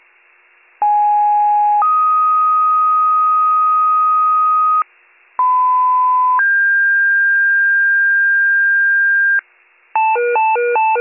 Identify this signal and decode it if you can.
Two-tone page, heard as a one-second tone, then a three-second tone, about 830 then 1250 Hz, about 1010 then 1600 Hz, with noise. It is heard from 0.8 to 9.4 s.